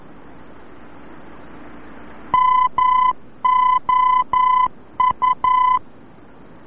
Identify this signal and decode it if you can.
Morse code, heard as a keyed tone, about 11 wpm, 1010 Hz. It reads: MOU
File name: france / mx / MOU-VOR-F